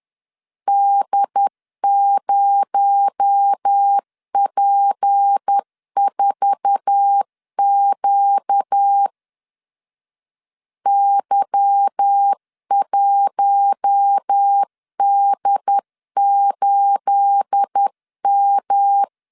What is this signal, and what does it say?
Morse code, heard as a keyed tone, about 10 wpm, 800 Hz. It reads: D0P4Q Y1D8M
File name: LEZIONI / 9 28 10wpm 2842